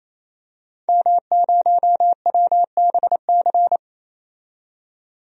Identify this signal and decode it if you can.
Morse code, heard as a keyed tone, about 28 wpm, 700 Hz. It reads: M0WBC
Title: DXCC-100 Call Signs - 5 Characters - 28wpm